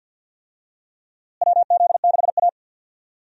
Morse code, heard as a keyed tone, about 51 wpm, 700 Hz. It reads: WZ6A
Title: USA Call Signs - 4 Characters - Repeated 2x - 50wpm